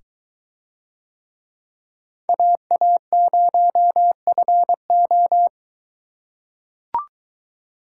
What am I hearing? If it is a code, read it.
Morse code, heard as a keyed tone, about 22 wpm, 700 Hz. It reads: AA0FO